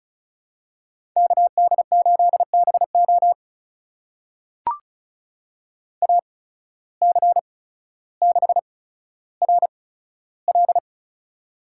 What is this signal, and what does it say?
Morse code, heard as a keyed tone, about 34 wpm, spaced as for 15 wpm, 700 Hz. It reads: KD8BO AC6RL
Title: USA Call Signs - 5 Characters - 8x Character Spacing - 35wpm